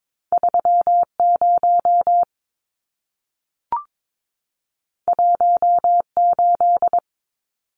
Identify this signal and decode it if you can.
Morse code, heard as a keyed tone, about 22 wpm, 700 Hz. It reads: 30 18
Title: Numbers - 2-Digit 22wpm